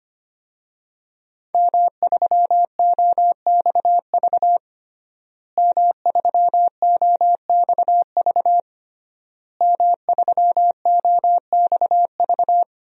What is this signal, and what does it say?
Morse code, heard as a keyed tone, about 25 wpm, 700 Hz. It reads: M3OXV M3OXV M3OXV
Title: DXCC-100 Call Signs - 5 Characters - Repeated 3x - 25wpm